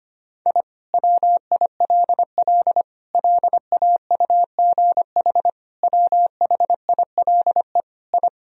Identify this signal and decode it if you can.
Morse code, heard as a keyed tone, about 25 wpm, 700 Hz. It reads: I WILL LAUGH WHILE I
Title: Sentences from Top 1000 Words - Review Part 1 - 25wpm